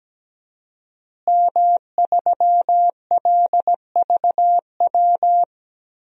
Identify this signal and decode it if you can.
Morse code, heard as a keyed tone, about 17 wpm, 700 Hz. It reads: M3LVW
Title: DXCC-100 Call Signs - 5 Characters - Repeated 3x - 17wpm